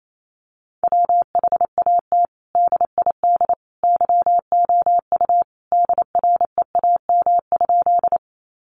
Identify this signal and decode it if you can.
Morse code, heard as a keyed tone, about 28 wpm, 700 Hz. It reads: WHAT DID YOU DREAM?
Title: Sentences from Top 1500 Words 28wpm